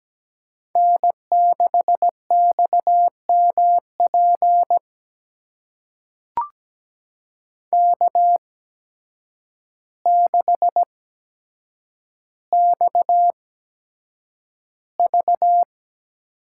Morse code, heard as a keyed tone, about 17 wpm, spaced as for 8 wpm, 700 Hz. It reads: N6XMP K6XV
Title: USA Call Signs - 8x Character Spacing - 17wpm